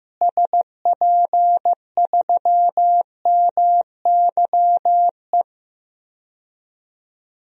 Morse code, heard as a keyed tone, about 16 wpm, 700 Hz. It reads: SP3MYE
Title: DXCC-100 Call Signs - 6 Characters - Repeated 3x - 15wpm